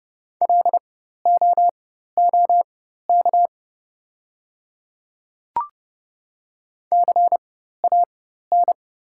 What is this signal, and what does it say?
Morse code, heard as a keyed tone, about 30 wpm, spaced as for 14 wpm, 700 Hz. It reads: LOOK CAN
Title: 4x Character Spacing - Top 200 Words - 30wpm